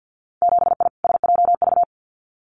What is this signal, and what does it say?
Morse code, heard as a keyed tone, about 50 wpm, 700 Hz. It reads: 7I SJV
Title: Winter Field Day Exchanges Encore 50wpm